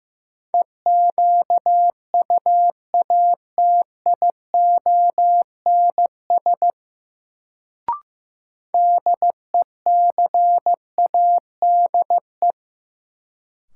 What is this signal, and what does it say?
Morse code, heard as a keyed tone, about 16 wpm, 700 Hz. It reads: EQUATIONS DECADE
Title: Sets of 5 Words - New Word Drill - 15wpm